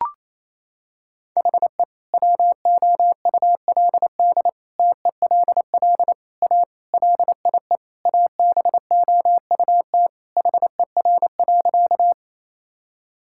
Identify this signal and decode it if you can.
Morse code, heard as a keyed tone, about 28 wpm, 700 Hz. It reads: HE WOULD TELL A LIE ABOUT HER.
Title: Sentences from Top 1600 Words 28wpm